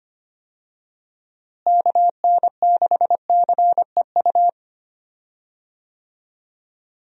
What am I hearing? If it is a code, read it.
Morse code, heard as a keyed tone, about 25 wpm, 700 Hz. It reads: KN6CEU
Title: USA Call Signs - 6 Characters - Repeated 2x - 25wpm